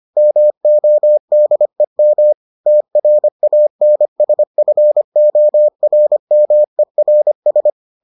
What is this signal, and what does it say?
Morse code, heard as a keyed tone, about 25 wpm, 600 Hz. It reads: MODEM TRANSFORMERS